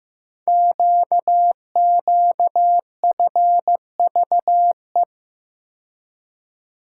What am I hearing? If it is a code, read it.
Morse code, heard as a keyed tone, about 15 wpm, 700 Hz. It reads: QQFVE
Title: Code Groups - 5 Letter:Numbers - 15wpm